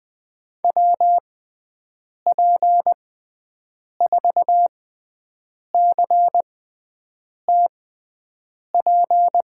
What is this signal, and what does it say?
Morse code, heard as a keyed tone, about 20 wpm, spaced as for 7 wpm, 700 Hz. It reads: WP4CTP
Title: USA Call Signs - 6 Characters - 6x Character Spacing - 20wpm